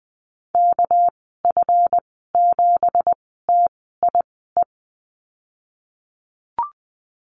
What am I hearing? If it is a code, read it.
Morse code, heard as a keyed tone, about 19 wpm, 700 Hz. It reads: K F 7 T I E
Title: 238 Lesson 38g - Letter q - Call Signs 20wpm